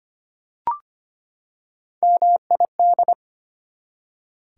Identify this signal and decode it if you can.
Morse code, heard as a keyed tone, about 25 wpm, 690 Hz. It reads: MID
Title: Prefixes and Suffixes 25wpm